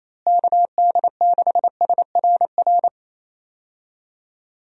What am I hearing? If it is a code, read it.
Morse code, heard as a keyed tone, about 28 wpm, 700 Hz. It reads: KD6SRR
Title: USA Call Signs - 6 Characters - Repeated 3x - 28wpm